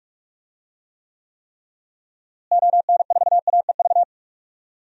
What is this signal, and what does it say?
Morse code, heard as a keyed tone, about 44 wpm, 700 Hz. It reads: ON4AEV
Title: DXCC-100 Call Signs - 6 Characters - Repeated 3x - 45wpm